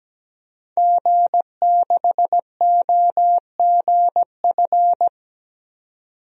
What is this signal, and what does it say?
Morse code, heard as a keyed tone, about 17 wpm, 700 Hz. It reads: G6OGF
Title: DXCC-100 Call Signs - 5 Characters - 17wpm